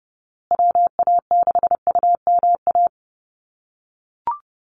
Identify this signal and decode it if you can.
Morse code, heard as a keyed tone, about 30 wpm, 700 Hz. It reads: WA6UMA
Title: USA Call Signs Repeated 3x 30wpm